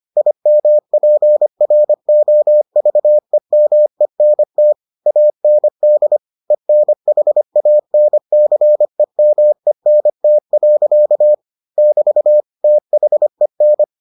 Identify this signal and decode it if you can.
Morse code, heard as a keyed tone, about 25 wpm, 600 Hz. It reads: IMPROVEMENT AND ENHANCEMENT. <BT> THEN